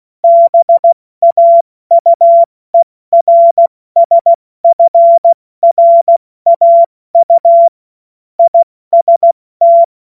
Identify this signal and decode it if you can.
Morse code, heard as a keyed tone, about 16 wpm, 680 Hz. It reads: BAUERSFRAU IST
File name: Morsekurs / Witze / Witz 27 Mai 2025 70 F80